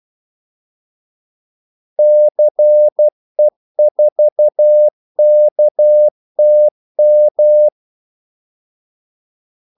Morse code, heard as a keyed tone, about 12 wpm, 600 Hz. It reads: CE4KTM
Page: 2 ekstra ordmellemrum mellem kaldesignalerne